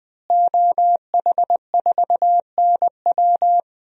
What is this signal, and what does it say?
Morse code, heard as a keyed tone, about 20 wpm, 700 Hz. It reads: OH4NW
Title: DXCC-100 Call Signs - 5 Characters - Repeated 3x - 20wpm